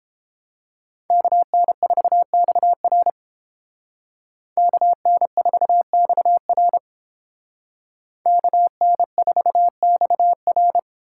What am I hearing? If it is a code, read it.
Morse code, heard as a keyed tone, about 30 wpm, 700 Hz. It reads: KN4XR KN4XR KN4XR